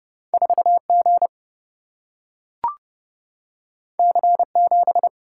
Morse code, heard as a keyed tone, about 30 wpm, 700 Hz. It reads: 4 G C 7
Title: Single-Letter Single-Number 30wpm